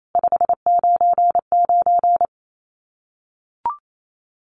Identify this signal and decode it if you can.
Morse code, heard as a keyed tone, about 28 wpm, 700 Hz. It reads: 599
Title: Common QSO Phrases 28wpm